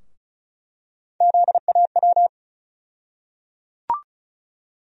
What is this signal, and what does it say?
Morse code, heard as a keyed tone, about 36 wpm, 700 Hz. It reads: ZAW
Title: USA Call Sign Postfixes - 3 Letters 35wpm